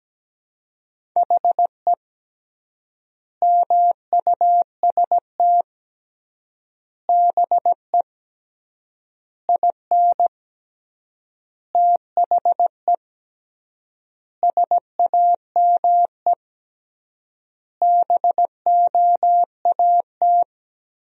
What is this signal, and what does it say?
Morse code, heard as a keyed tone, about 17 wpm, 700 Hz. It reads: HE MUST BE IN THE SAME BOAT
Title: 3x Word-Spacing - Sentences from Top 500 Words - 17wpm